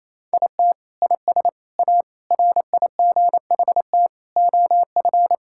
Morse code, heard as a keyed tone, about 28 wpm, 700 Hz. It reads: IT IS A RIGHT OF
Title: Sentences from Top 1000 Words - Review Part 5 - 28wpm